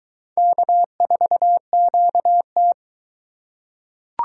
Morse code, heard as a keyed tone, about 20 wpm, 700 Hz. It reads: K4QT